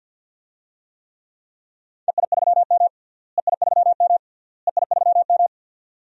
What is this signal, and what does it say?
Morse code, heard as a keyed tone, about 49 wpm, 700 Hz. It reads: EI3M EI3M EI3M
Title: DXCC-100 Call Signs - 4 Characters - Repeated 3x - 50wpm